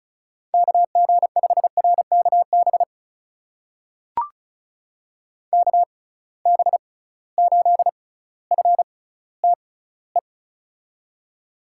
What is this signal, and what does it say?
Morse code, heard as a keyed tone, about 36 wpm, spaced as for 18 wpm, 700 Hz. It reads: KG5RKB KB8FTE
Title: USA Call Signs - 6 Characters - 6x Character Spacing - 35wpm